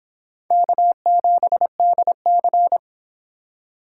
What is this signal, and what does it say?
Morse code, heard as a keyed tone, about 30 wpm, 700 Hz. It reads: K7DC